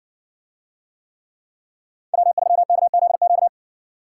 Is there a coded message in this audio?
Yes: Morse code from 2.1 to 3.5 s, heard as a keyed tone, about 55 wpm, 700 Hz. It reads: W3KZQ